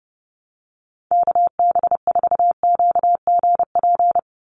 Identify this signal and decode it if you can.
Morse code, heard as a keyed tone, about 30 wpm, 700 Hz. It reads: KB4QGP
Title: USA Call Signs 30wpm